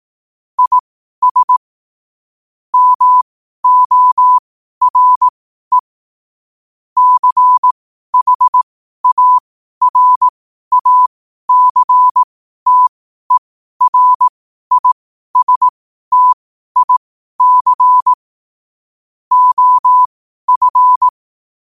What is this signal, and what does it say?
Morse code, heard as a keyed tone, about 18 wpm, spaced as for 12 wpm, 1000 Hz. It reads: IS MORE CHARACTERISTIC OF
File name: qotdfiles / 12wpm